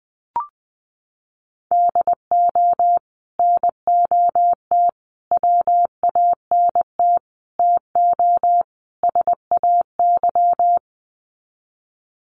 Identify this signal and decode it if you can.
Morse code, heard as a keyed tone, about 20 wpm, 700 Hz. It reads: DO NOT WANT TO SAY
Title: Sets of 5 Words Encore 20wpm